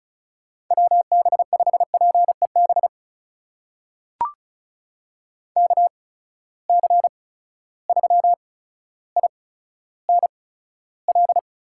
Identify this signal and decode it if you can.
Morse code, heard as a keyed tone, about 36 wpm, spaced as for 15 wpm, 700 Hz. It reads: WB5PEB KC3INL
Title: USA Call Signs - 6 Characters - 8x Character Spacing - 35wpm